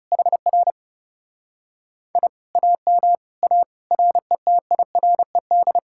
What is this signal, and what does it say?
Morse code, heard as a keyed tone, about 41 wpm, spaced as for 36 wpm, 700 Hz. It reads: HR I AM A RETIRED
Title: Speed-Racing Common QSO Phrases - 30wpm